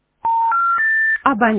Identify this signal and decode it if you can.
Special information tone, with speech.